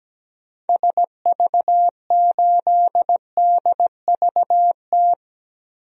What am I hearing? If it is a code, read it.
Morse code, heard as a keyed tone, about 17 wpm, 700 Hz. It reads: SV8DVT
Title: DXCC-100 Call Signs - 6 Characters - Repeated 3x - 17wpm